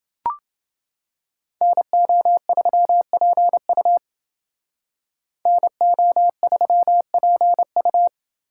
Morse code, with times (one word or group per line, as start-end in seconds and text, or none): 1.6-4.0 NO3PU
5.5-8.1 NO3PU